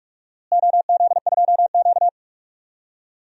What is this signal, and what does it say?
Morse code, heard as a keyed tone, about 42 wpm, 700 Hz. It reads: OZ2Q
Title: DXCC-100 Call Signs - 4 Characters - Repeated 2x - 45wpm